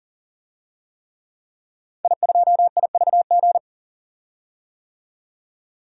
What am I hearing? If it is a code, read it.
Morse code, heard as a keyed tone, about 40 wpm, 700 Hz. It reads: I2IVG
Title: DXCC-100 Call Signs - 5 Characters - 40wpm